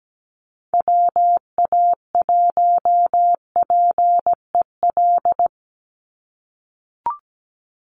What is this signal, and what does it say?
Morse code, heard as a keyed tone, about 17 wpm, 700 Hz. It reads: WA1PEL
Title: USA Call Signs 17wpm